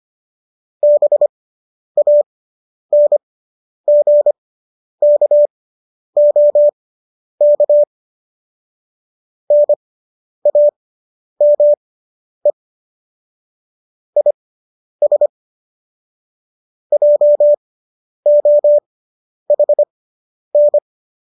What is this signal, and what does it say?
Morse code, heard as a keyed tone, about 25 wpm, spaced as for 10 wpm, 590 Hz. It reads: BANGKOK NAME IS JOHN